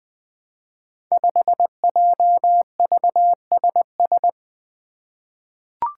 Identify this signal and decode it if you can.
Morse code, heard as a keyed tone, about 20 wpm, 700 Hz. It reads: 5JVSS
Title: Five Challenging Letter-Number Combinations 20wpm